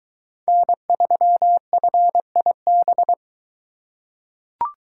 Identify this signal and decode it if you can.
Morse code, heard as a keyed tone, about 22 wpm, 700 Hz. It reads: N3FIB